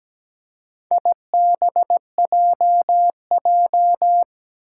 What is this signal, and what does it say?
Morse code, heard as a keyed tone, about 17 wpm, 700 Hz. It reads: IBJJ
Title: Four Challenging Letter-Number Combinations 17wpm